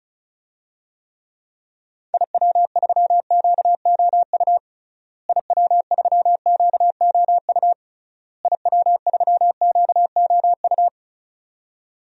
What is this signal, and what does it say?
Morse code, heard as a keyed tone, about 34 wpm, 700 Hz. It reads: IW3QOU IW3QOU IW3QOU
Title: DXCC-100 Call Signs - 6 Characters - Repeated 3x - 35wpm